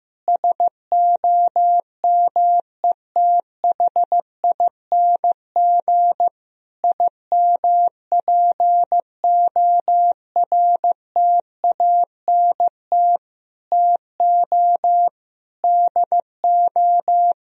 Morse code, heard as a keyed tone, about 16 wpm, 700 Hz. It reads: SOMETHING IMPORTANT TO DO
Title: Sentences from Top 1000 Words - Review Part 4 - 15wpm